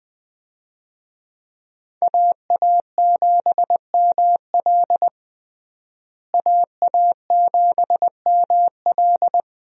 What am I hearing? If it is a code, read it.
Morse code, heard as a keyed tone, about 20 wpm, 700 Hz. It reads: AA7ML AA7ML
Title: USA Call Signs - 5 Characters - Repeated 3x - 20wpm